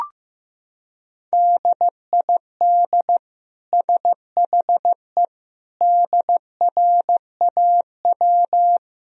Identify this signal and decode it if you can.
Morse code, heard as a keyed tone, about 16 wpm, 700 Hz. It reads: DID SHE DRAW
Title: Sentences from Top 1000 Words - Review Part 2 - 15wpm